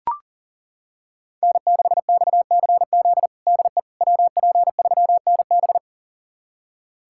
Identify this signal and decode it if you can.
Morse code, heard as a keyed tone, about 40 wpm, 700 Hz. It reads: N6XCZ DE WP3NB
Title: USA Callsign DE Callsign - 5 Characters - 40wpm